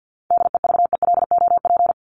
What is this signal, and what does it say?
Morse code, heard as a keyed tone, about 50 wpm, 700 Hz. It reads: DEVELOP
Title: Top 500 Words Repeated 50wpm